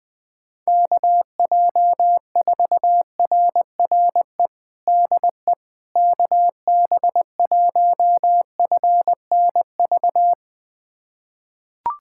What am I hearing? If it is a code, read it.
Morse code, heard as a keyed tone, about 21 wpm, 700 Hz. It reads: KJ4RRE DE KB1FNV
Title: USA Callsign DE Callsign - 6 Characters - 20wpm